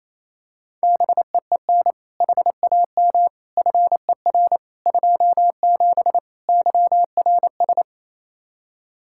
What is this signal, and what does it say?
Morse code, heard as a keyed tone, about 28 wpm, 700 Hz. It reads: BEEN HAM FER 27 YRS
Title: Practice QSO - Exchange 3 - More Info - 28wpm